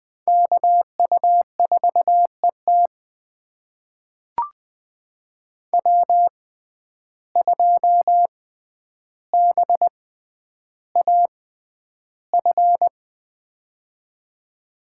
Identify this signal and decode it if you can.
Morse code, heard as a keyed tone, about 20 wpm, spaced as for 10 wpm, 700 Hz. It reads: KU4ET W2BAF
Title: USA Call Signs - 5 Characters - 6x Character Spacing - 20wpm